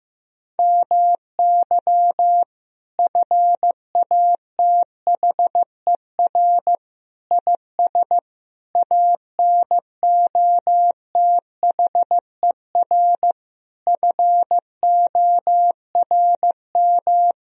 Morse code, heard as a keyed tone, about 15 wpm, 700 Hz. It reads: MY FATHER IS ANOTHER FORM
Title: Sentences from Top 1000 Words - Review Part 1 - 15wpm